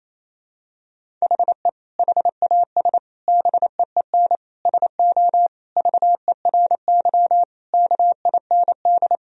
Morse code, heard as a keyed tone, about 27 wpm, 700 Hz. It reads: HE HAS BEEN SO VERY KIND
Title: Sentences from Top 1000 Words - Review Part 1 - 28wpm